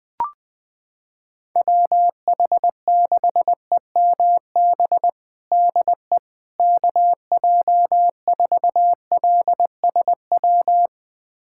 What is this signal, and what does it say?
Morse code, heard as a keyed tone, about 20 wpm, 700 Hz. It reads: WH6EMB DE KJ4LSW
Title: USA Callsign DE Callsign 20wpm